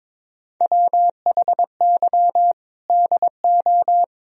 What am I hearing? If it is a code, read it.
Morse code, heard as a keyed tone, about 22 wpm, 700 Hz. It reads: WHY DO
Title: Sentences from Top 300 Words and Abbreviations 22wpm